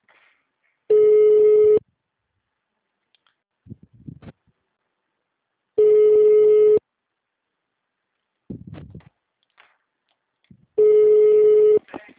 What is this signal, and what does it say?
Ringback tone (Europe, 425 Hz), with speech.
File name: out- -2002- - - .1039